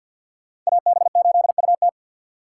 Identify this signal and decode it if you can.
Morse code, heard as a keyed tone, about 62 wpm, 700 Hz. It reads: AB8UT
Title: Speed Racing - USA Call Signs 50wpm